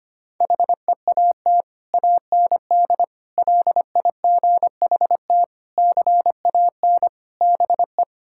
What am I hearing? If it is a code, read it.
Morse code, heard as a keyed tone, about 25 wpm, 700 Hz. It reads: HEAT AND LIGHT CAN BE
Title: Sentences from Top 1000 Words - Review Part 6 - 25wpm